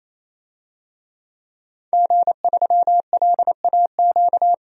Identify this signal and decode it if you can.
Morse code, heard as a keyed tone, about 28 wpm, 700 Hz. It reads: G3LAQ
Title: DXCC-100 Call Signs - 5 Characters - 28wpm